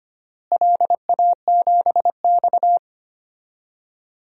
Morse code, heard as a keyed tone, about 25 wpm, 700 Hz. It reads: LA7X
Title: DXCC-100 Call Signs - 4 Characters - Repeated 2x - 25wpm